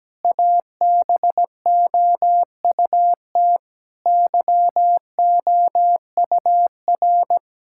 Morse code, heard as a keyed tone, about 17 wpm, 700 Hz. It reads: ABOUT YOUR
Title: Sentences from Top 1000 Words - Review Part 6 - 17wpm